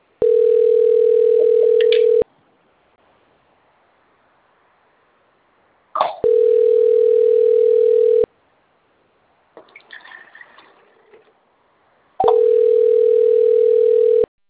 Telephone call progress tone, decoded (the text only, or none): ringback tone